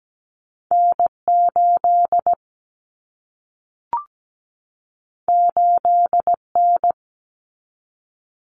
Morse code, heard as a keyed tone, about 17 wpm, 700 Hz. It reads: N8 8N
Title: Common Cut Numbers - 2-Digit 17wpm